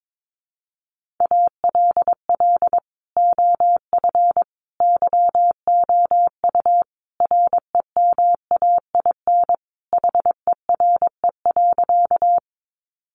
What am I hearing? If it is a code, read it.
Morse code, heard as a keyed tone, about 22 wpm, 700 Hz. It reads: ALL OF YOU REMAIN HERE.
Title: Sentences from Top 1200 Words 22wpm